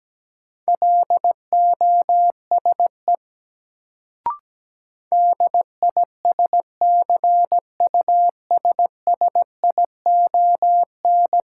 Morse code, heard as a keyed tone, about 17 wpm, 700 Hz. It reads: LOSE DISCUSSION
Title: Sentences from Top 1100 Words - New Word Drill - 17wpm